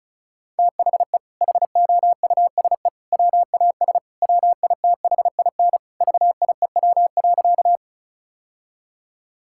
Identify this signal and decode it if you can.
Morse code, heard as a keyed tone, about 36 wpm, 700 Hz. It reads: THE HOUSE WAS WITHIN VIEW.
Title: Sentences from Top 1000 Words - Review Part 6 - 35wpm